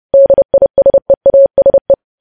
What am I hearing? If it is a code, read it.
Morse code, heard as a keyed tone, about 30 wpm, 550 Hz. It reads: DISEASE
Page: news headlines in morse code at 30 wpm